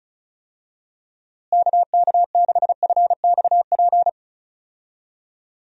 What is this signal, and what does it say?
Morse code, heard as a keyed tone, about 34 wpm, 700 Hz. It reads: KK6FXP
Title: USA Call Signs - 6 Characters - Repeated 2x - 35wpm